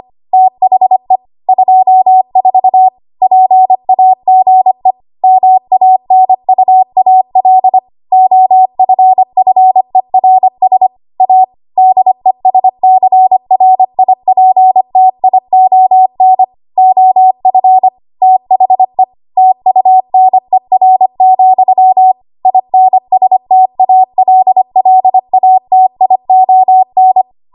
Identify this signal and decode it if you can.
Morse code, heard as a keyed tone, about 25 wpm, 750 Hz. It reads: THE 24 PAGE MANUAL OFFERS A DESCRIPTION OF THE TUNER, INSTALLATION
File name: Archive / 25 WPM / 25WPM